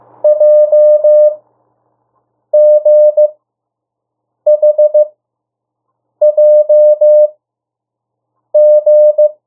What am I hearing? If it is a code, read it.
Morse code, heard as a keyed tone, about 15 wpm, spaced as for 6 wpm, 600 Hz. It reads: JGHJG
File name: urok 1